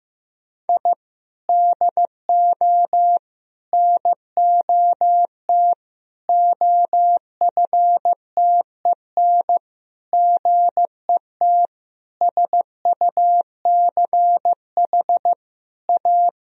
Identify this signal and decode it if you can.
Morse code, heard as a keyed tone, about 15 wpm, 700 Hz. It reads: I DO NOT OFTEN GET SUCH A
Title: Sentences from Top 1000 Words - Review Part 1 - 15wpm